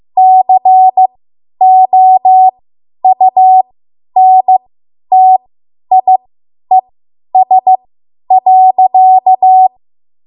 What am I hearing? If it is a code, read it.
Morse code, heard as a keyed tone, about 15 wpm, spaced as for 10 wpm, 750 Hz. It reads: COUNTIES.
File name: Morse / 10WPM